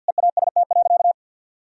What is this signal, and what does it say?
Morse code, heard as a keyed tone, about 47 wpm, 700 Hz. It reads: EAST.
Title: Sentences from Top 1000 Words - Review Part 3 - 50wpm